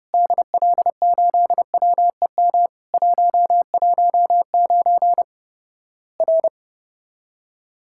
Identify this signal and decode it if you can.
Morse code, heard as a keyed tone, about 30 wpm, 700 Hz. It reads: DL8WEM 119 R
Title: Worked All Europe DX Contest - QTC Exchange - 30wpm